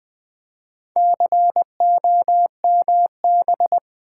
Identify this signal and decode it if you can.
Morse code, heard as a keyed tone, about 20 wpm, 700 Hz. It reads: COMB
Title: Easy Words with Hints - 20wpm